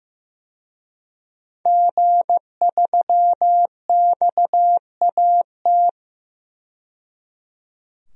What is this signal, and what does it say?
Morse code, heard as a keyed tone, about 15 wpm, 700 Hz. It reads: G3XAT
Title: DXCC-100 Call Signs - 5 Characters - 15wpm